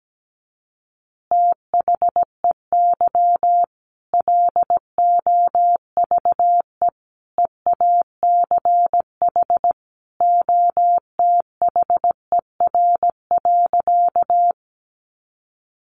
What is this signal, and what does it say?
Morse code, heard as a keyed tone, about 17 wpm, 700 Hz. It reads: THEY LOVE EACH OTHER.
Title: Easy Phrases 17wpm